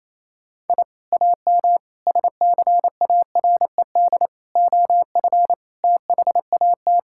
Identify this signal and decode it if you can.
Morse code, heard as a keyed tone, about 27 wpm, 700 Hz. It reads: I AM SCARED OF THAT
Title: Sentences from Top 1000 Words - Review Part 5 - 28wpm